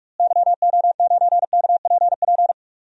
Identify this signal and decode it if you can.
Morse code, heard as a keyed tone, about 41 wpm, 700 Hz. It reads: YO9KPP
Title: DXCC-100 Call Signs - 6 Characters - Repeated 2x - 45wpm